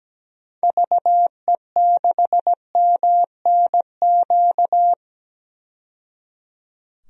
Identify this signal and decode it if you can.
Morse code, heard as a keyed tone, about 17 wpm, 700 Hz. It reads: VE6MNQ
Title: DXCC-100 Call Signs - 6 Characters - 17wpm